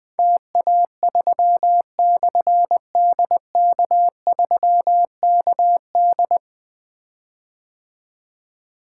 Morse code, heard as a keyed tone, about 20 wpm, 700 Hz. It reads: TA3/DK3KD
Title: Difficult Call Signs 20wpm